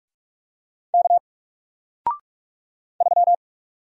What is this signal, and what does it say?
Morse code, heard as a keyed tone, about 44 wpm, 700 Hz. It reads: K 3
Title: Single Character (Letter:Number:Punctuation:Prosign) Alt 45wpm